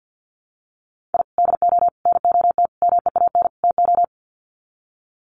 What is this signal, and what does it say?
Morse code, heard as a keyed tone, about 50 wpm, 700 Hz. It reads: I DO NOT MEAN TO
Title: Sets of 5 Words 50wpm